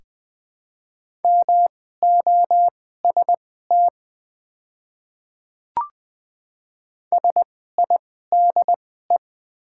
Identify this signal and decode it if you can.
Morse code, heard as a keyed tone, about 21 wpm, spaced as for 15 wpm, 700 Hz. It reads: MOST SIDE
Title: 2x Character Spacing - Top 300 Words - 20wpm